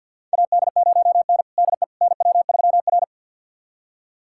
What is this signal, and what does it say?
Morse code, heard as a keyed tone, about 47 wpm, 700 Hz. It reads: AD0N DE NW3R
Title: USA Callsign DE Callsign - 4 Characters - 50wpm